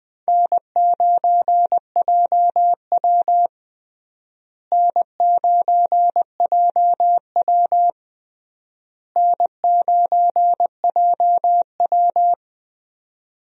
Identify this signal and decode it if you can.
Morse code, heard as a keyed tone, about 21 wpm, 700 Hz. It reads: N9JW N9JW N9JW
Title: USA Call Signs - 4 Characters - Repeated 3x - 20wpm